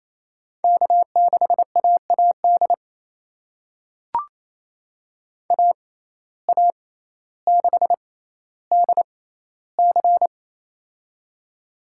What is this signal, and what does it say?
Morse code, heard as a keyed tone, about 28 wpm, spaced as for 14 wpm, 700 Hz. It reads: K6AAD AA6DC
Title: USA Call Signs - 6x Character Spacing - 28wpm